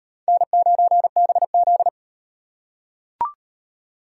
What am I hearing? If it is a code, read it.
Morse code, heard as a keyed tone, about 37 wpm, 700 Hz. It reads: N9BZ